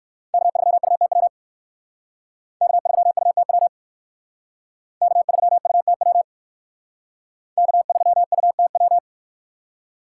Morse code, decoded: K3UTW K3UTW K3UTW K3UTW

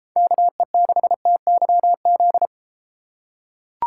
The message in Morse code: KE6TYZ

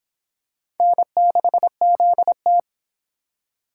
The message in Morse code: N6ZT

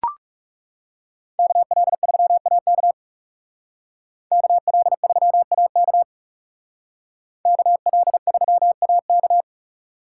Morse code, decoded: KL3AK KL3AK KL3AK

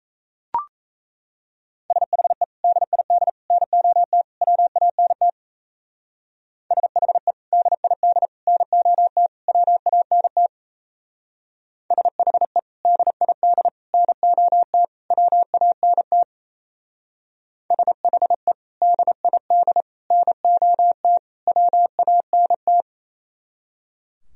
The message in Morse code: SHE DID NOT WANT SHE DID NOT WANT SHE DID NOT WANT SHE DID NOT WANT

